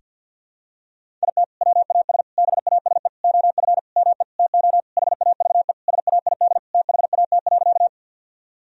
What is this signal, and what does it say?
Morse code, heard as a keyed tone, about 52 wpm, 700 Hz. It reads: IT WAS BASE OF ME TO HAVE SAID THAT.
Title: Sentences from Top 1000 Words - Review Part 1 - 50wpm